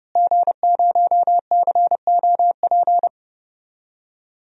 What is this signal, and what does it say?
Morse code, heard as a keyed tone, about 30 wpm, 700 Hz. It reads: G0COP
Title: DXCC-100 Call Signs - 5 Characters - Repeated 3x - 30wpm